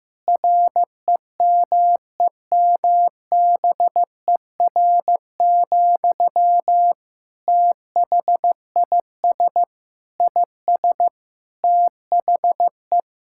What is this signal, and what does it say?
Morse code, heard as a keyed tone, about 15 wpm, 700 Hz. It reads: REMEMBER, THIS IS THE
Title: Sentences from Top 1000 Words - Review Part 5 - 15wpm